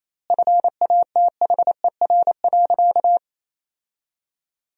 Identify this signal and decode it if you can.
Morse code, heard as a keyed tone, about 28 wpm, 700 Hz. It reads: FATHER.
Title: 2x Word-Spacing - Sentences from Top 300 Words - 28wpm